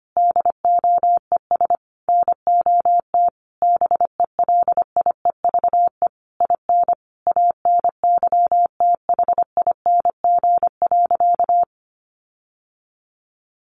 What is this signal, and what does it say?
Morse code, heard as a keyed tone, about 25 wpm, 700 Hz. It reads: DOES NOT BELIEVE IN ANYTHING.
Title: Sentences from Top 1500 Words 25wpm